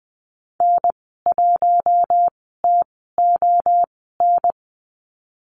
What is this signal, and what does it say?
Morse code, heard as a keyed tone, about 20 wpm, spaced as for 14 wpm, 700 Hz. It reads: N1TON
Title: 076 Lesson 13g - Letter L - Call Signs 20wpm